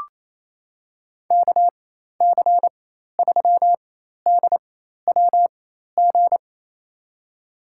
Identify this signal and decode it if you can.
Morse code, heard as a keyed tone, about 28 wpm, spaced as for 13 wpm, 700 Hz. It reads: KC3DWG
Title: USA Call Signs - 6 Characters - 4x Character Spacing - 28wpm